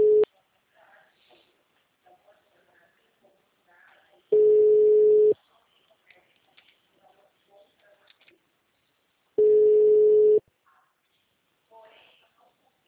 Ringback tone (Europe, 425 Hz).
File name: out- -2030- - - .15213